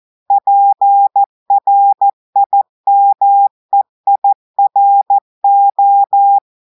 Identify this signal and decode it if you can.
Morse code, heard as a keyed tone, about 14 wpm, 800 Hz. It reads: PRIMEIRO
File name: audios / portugais / portugais texte en clair 1 vitesse 14 mots minutes